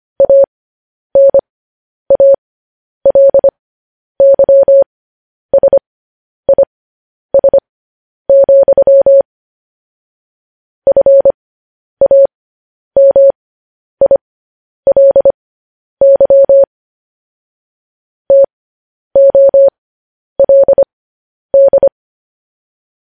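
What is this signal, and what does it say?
Morse code, heard as a keyed tone, about 25 wpm, spaced as for 10 wpm, 550 Hz. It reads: ANALYSIS, FAMILY TOLD
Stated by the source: Morse code news headlines practice transmission.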